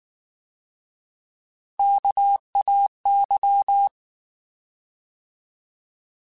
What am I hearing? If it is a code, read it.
Morse code, heard as a keyed tone, about 19 wpm, 800 Hz. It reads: KAY